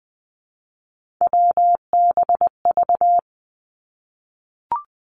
Morse code, heard as a keyed tone, about 20 wpm, 700 Hz. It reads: WBV
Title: Three Challenging Letter-Number Combinations 20wpm